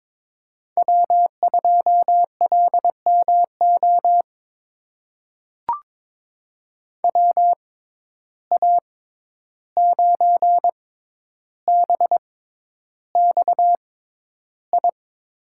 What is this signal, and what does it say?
Morse code, heard as a keyed tone, about 22 wpm, spaced as for 11 wpm, 700 Hz. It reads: W2LMO WA9BXI
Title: USA Call Signs - 6x Character Spacing - 22wpm